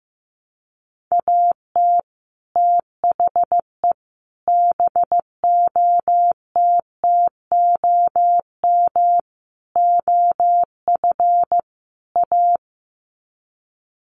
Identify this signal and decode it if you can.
Morse code, heard as a keyed tone, about 15 wpm, 700 Hz. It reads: AT THE BOTTOM OF A
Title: Sets of 5 Words Encore2 15wpm